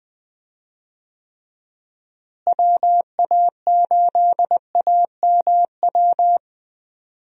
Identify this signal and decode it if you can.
Morse code, heard as a keyed tone, about 20 wpm, 700 Hz. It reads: WA8AMW
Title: USA Call Signs - 6 Characters - Repeated 3x - 20wpm